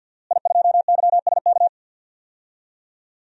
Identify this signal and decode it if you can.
Morse code, heard as a keyed tone, about 48 wpm, 700 Hz. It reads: I2YSK
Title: DXCC-100 Call Signs - 5 Characters - Repeated 2x - 50wpm